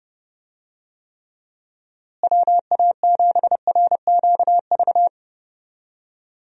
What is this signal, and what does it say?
Morse code, heard as a keyed tone, about 30 wpm, 700 Hz. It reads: WA7RQV
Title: USA Call Signs - 6 Characters - Repeated 3x - 30wpm